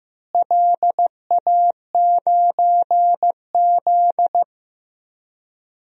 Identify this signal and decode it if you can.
Morse code, heard as a keyed tone, about 15 wpm, 700 Hz. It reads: LA9Z
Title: DXCC-100 Call Signs - 4 Characters - Repeated 2x - 15wpm